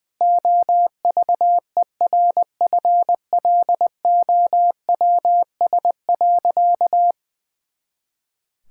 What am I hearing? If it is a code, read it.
Morse code, heard as a keyed tone, about 20 wpm, 700 Hz. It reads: OVERFLOWS.
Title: Pangram Challenge v2 - 20wpm